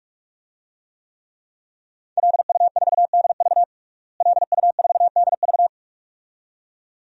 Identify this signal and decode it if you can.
Morse code, heard as a keyed tone, about 46 wpm, 700 Hz. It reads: LU4DV LU4DV
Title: DXCC-100 Call Signs - 5 Characters - Repeated 2x - 45wpm